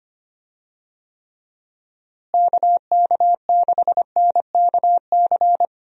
Morse code, heard as a keyed tone, about 25 wpm, 700 Hz. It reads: KK6NKC
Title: USA Call Signs - 6 Characters - Repeated 3x - 25wpm